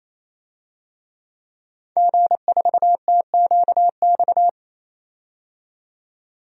Morse code, heard as a keyed tone, about 28 wpm, 700 Hz. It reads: G4TQX
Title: DXCC-100 Call Signs - 5 Characters - 28wpm